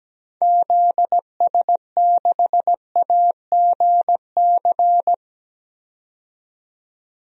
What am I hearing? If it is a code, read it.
Morse code, heard as a keyed tone, about 17 wpm, 700 Hz. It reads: ZS6AGC
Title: DXCC-100 Call Signs - 6 Characters - 17wpm